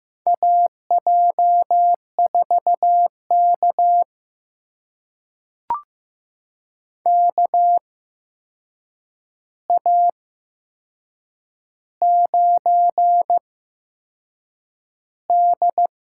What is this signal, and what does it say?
Morse code, heard as a keyed tone, about 15 wpm, spaced as for 6 wpm, 700 Hz. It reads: AJ4K KA9D
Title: USA Call Signs - 4 Characters - 8x Character Spacing - 15wpm